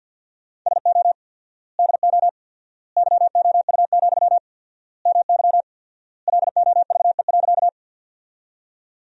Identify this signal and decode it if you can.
Morse code, heard as a keyed tone, about 50 wpm, 700 Hz. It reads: SO DO YOU, MY LOVE.
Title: 4x Word-Spacing - Sentences from Top 400 Words - 50wpm